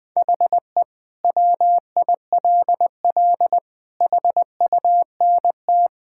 Morse code, heard as a keyed tone, about 20 wpm, 700 Hz. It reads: HE WILL HUNT
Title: Sentences from Top 1600 Words 20wpm